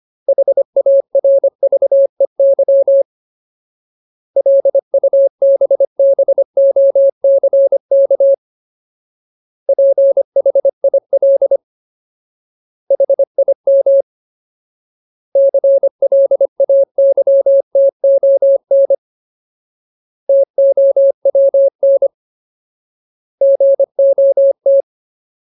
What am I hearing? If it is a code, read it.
Morse code, heard as a keyed tone, about 25 wpm, 550 Hz. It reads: HARVEY LUBBOCK PHIL HIM CLAYTON TOWN GOT
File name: Ultimate Word List / 25WPM Ultimate